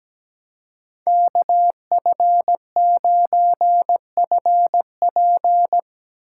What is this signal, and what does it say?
Morse code, heard as a keyed tone, about 17 wpm, 700 Hz. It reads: KF9FP
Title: USA Call Signs - 5 Characters - 6x Character Spacing - 17wpm